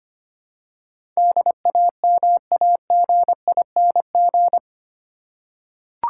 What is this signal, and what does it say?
Morse code, heard as a keyed tone, about 25 wpm, 700 Hz. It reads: DAMAGING
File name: Eight Letter Words - 25wpm